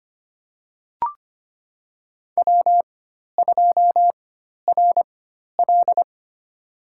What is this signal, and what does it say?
Morse code, heard as a keyed tone, about 25 wpm, spaced as for 12 wpm, 700 Hz. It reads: W2RL
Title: USA Call Signs - 4 Characters - 4x Character Spacing - 25wpm